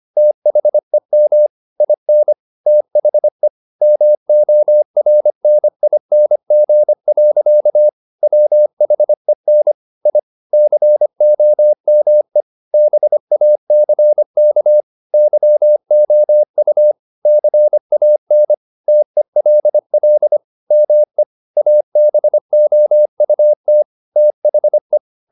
Morse code, heard as a keyed tone, about 25 wpm, 600 Hz. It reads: THEM IN THE MORNING. WHEN I COME BACK YOU CAN TELL ME ABOUT THE
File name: the old man and the sea part 1 25wpm